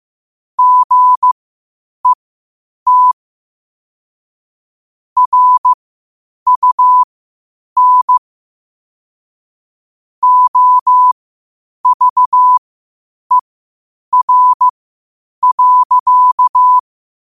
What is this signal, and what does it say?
Morse code, heard as a keyed tone, about 15 wpm, spaced as for 8 wpm, 1000 Hz. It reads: GET RUN OVER.
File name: qotdfiles / 8wpm